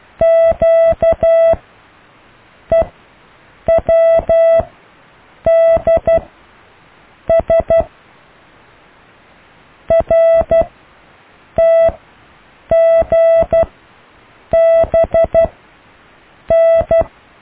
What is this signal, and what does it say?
Morse code, heard as a keyed tone, about 12 wpm, spaced as for 6 wpm, 660 Hz. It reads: QEWDS RTGBN
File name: cw / Alle letters